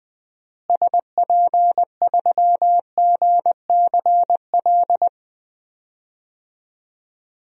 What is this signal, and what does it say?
Morse code, heard as a keyed tone, about 20 wpm, 700 Hz. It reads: SP3GCL
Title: DXCC-100 Call Signs - 6 Characters - Repeated 2x - 20wpm